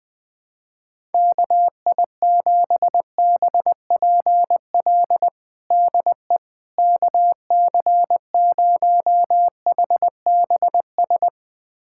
Morse code, heard as a keyed tone, about 20 wpm, 700 Hz. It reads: KI7BPL DE KC0HBS
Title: USA Callsign DE Callsign - 6 Characters - 20wpm